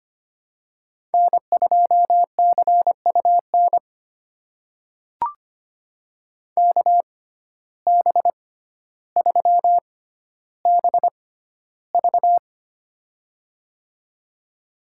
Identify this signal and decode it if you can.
Morse code, heard as a keyed tone, about 25 wpm, spaced as for 13 wpm, 700 Hz. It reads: N2CUN KB3BV
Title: USA Call Signs - 5 Characters - 6x Character Spacing - 25wpm